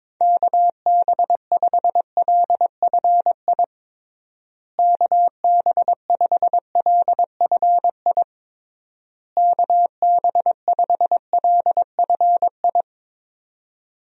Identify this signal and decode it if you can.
Morse code, heard as a keyed tone, about 22 wpm, 700 Hz. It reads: KB5LFI KB5LFI KB5LFI
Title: USA Call Signs - 6 Characters - Repeated 3x - 22wpm